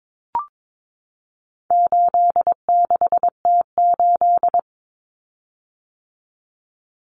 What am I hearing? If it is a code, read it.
Morse code, heard as a keyed tone, about 22 wpm, 700 Hz. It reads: 86T8
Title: Common Cut Numbers - 4-Digit 22wpm